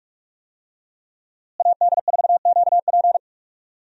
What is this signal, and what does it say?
Morse code, heard as a keyed tone, about 41 wpm, 700 Hz. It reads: AD4QP
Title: USA Call Signs - 5 Characters - Repeated 3x - 45wpm